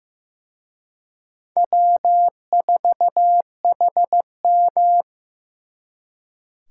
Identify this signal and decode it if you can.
Morse code, heard as a keyed tone, about 18 wpm, 700 Hz. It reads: W4HM
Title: Speed Racing - USA Call Signs 15wpm